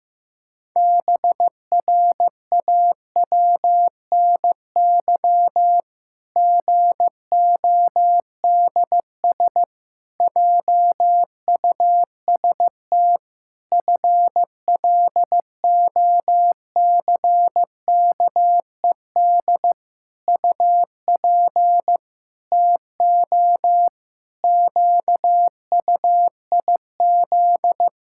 Morse code, heard as a keyed tone, about 16 wpm, 700 Hz. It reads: BRAWNY GODS JUST FLOCKED UP TO QUIZ
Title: Pangram Challenge v2 - 15wpm